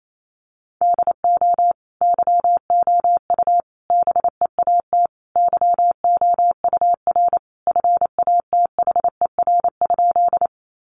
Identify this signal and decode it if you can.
Morse code, heard as a keyed tone, about 28 wpm, 700 Hz. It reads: DO YOU BEAT YOUR FATHER?
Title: Sentences from Top 1300 Words 28wpm